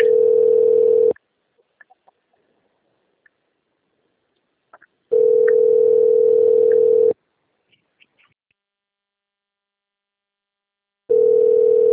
Ringback tone (North America), 440 + 480 Hz.